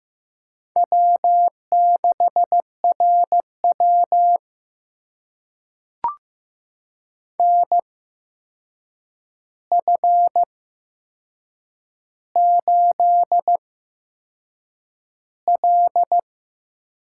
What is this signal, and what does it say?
Morse code, heard as a keyed tone, about 15 wpm, spaced as for 6 wpm, 700 Hz. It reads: W6RW NF8L
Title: USA Call Signs - 8x Character Spacing - 15wpm